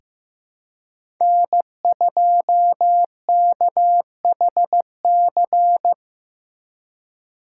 Morse code, heard as a keyed tone, about 15 wpm, 700 Hz. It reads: N2KHC
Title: USA Call Signs - 2x Character Spacing - 15wpm